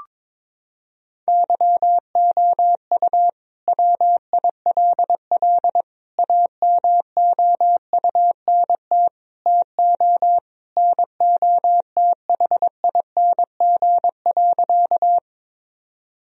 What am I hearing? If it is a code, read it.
Morse code, heard as a keyed tone, about 22 wpm, 700 Hz. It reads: YOU WILL AMOUNT TO NOTHING.
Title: Sentences from Top 1000 Words - Review Part 4 - 22wpm